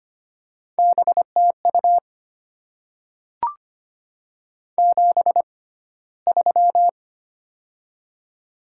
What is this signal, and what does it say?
Morse code, heard as a keyed tone, about 25 wpm, spaced as for 15 wpm, 700 Hz. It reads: BTU 73
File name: Top 20 QSO Elements - 6x Character Spacing - 25wpm